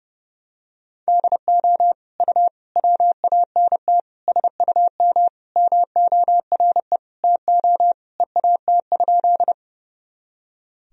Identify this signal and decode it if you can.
Morse code, heard as a keyed tone, about 30 wpm, 700 Hz. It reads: DO U WANT SUM MORE TO EAT?
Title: Sentences from Top 300 Words and Abbreviations 30wpm